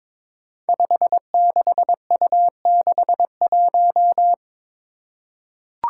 Morse code, heard as a keyed tone, about 22 wpm, 700 Hz. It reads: 56U61
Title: Five Challenging Letter-Number Combinations 22wpm